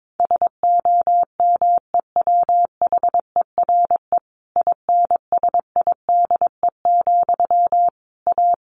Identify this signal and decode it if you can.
Morse code, heard as a keyed tone, about 22 wpm, 700 Hz. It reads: SOMEWHERE INSIDE, A
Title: A Sound Of Thunder 22wpm